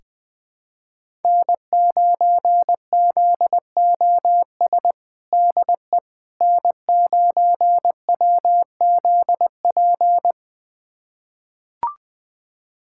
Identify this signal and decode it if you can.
Morse code, heard as a keyed tone, about 20 wpm, 700 Hz. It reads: N9ZOS DE N9WZP
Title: USA Callsign DE Callsign - 5 Characters - 20wpm